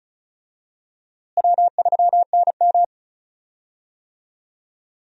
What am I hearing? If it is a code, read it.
Morse code, heard as a keyed tone, about 36 wpm, 700 Hz. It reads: W3NM
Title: USA Call Signs - 4 Characters - Repeated 3x - 35wpm